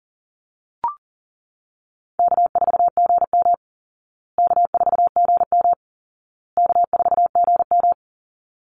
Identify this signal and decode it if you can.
Morse code, heard as a keyed tone, about 40 wpm, 700 Hz. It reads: K4GM K4GM K4GM
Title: Easy Call Signs Repeated 3x - 40wpm